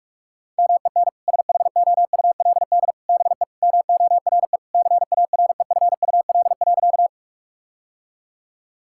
Morse code, heard as a keyed tone, about 46 wpm, 700 Hz. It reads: MEN SHOULD BE MORE CAREFUL.
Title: Sentences from Top 1600 Words 45wpm